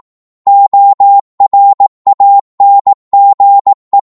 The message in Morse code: ORANGE